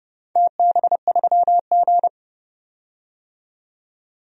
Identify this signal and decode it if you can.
Morse code, heard as a keyed tone, about 30 wpm, 700 Hz. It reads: TB3G
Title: DXCC-100 Call Signs - 4 Characters - 30wpm